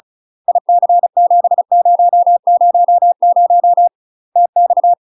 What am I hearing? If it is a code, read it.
Morse code, heard as a keyed tone, about 34 wpm, 700 Hz. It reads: IC7000 TX
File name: UK MASTER QSOs TRAINING / 35WPM / UK MASTER QSOs 0009